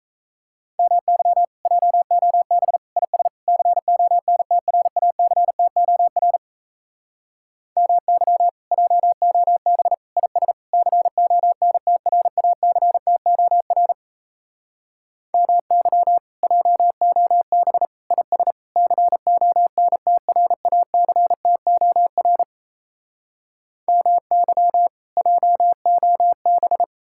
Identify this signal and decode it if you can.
Morse code, heard as a keyed tone, about 38 wpm, 700 Hz. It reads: MY JOB IS CONTRACTOR MY JOB IS CONTRACTOR MY JOB IS CONTRACTOR MY JOB